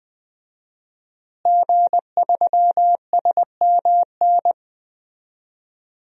Morse code, heard as a keyed tone, about 21 wpm, 700 Hz. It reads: G3SMN
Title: DXCC-100 Call Signs - 5 Characters - Repeated 3x - 20wpm